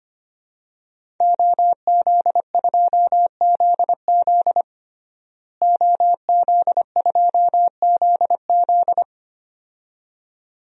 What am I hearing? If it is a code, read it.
Morse code, heard as a keyed tone, about 25 wpm, 700 Hz. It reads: OZ2ZZ OZ2ZZ
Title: DXCC-100 Call Signs - 5 Characters - Repeated 2x - 25wpm